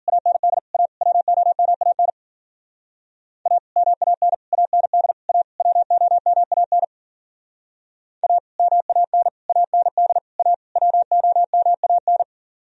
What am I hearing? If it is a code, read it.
Morse code, heard as a keyed tone, about 49 wpm, 700 Hz. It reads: AND A WOMAN A MAN AND A WOMAN A MAN AND A WOMAN
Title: Speed-Racing - Sets of 5 Words 40wpm